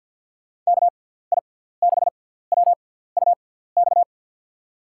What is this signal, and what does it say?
Morse code, heard as a keyed tone, about 48 wpm, spaced as for 17 wpm, 700 Hz. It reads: KI6WUX
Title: USA Call Signs - 6 Characters - 6x Character Spacing - 50wpm